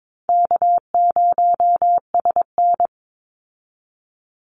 Morse code, heard as a keyed tone, about 22 wpm, 700 Hz. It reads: K0SN
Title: Call Signs Repeated 3x - 22wpm